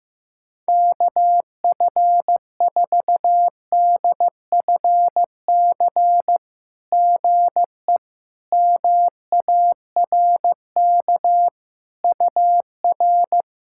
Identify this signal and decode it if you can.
Morse code, heard as a keyed tone, about 16 wpm, 700 Hz. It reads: KF4DFC GE MARK UR
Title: SOTA QSO Practice 15wpm